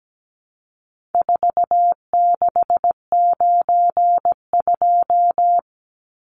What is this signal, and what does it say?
Morse code, heard as a keyed tone, about 17 wpm, 700 Hz. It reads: 4692
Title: Numbers - 4-Digit 17wpm